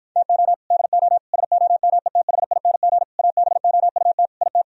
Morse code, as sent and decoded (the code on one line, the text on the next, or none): - --- / -.. --- / ... --- -- . - .... .. -. --. / .- -... --- ..- - / .. -
TO DO SOMETHING ABOUT IT